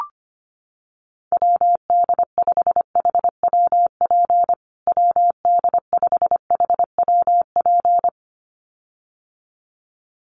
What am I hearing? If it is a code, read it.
Morse code, heard as a keyed tone, about 25 wpm, 700 Hz. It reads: WD5HWP WD5HWP
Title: USA Call Signs Repeated 2x 25wpm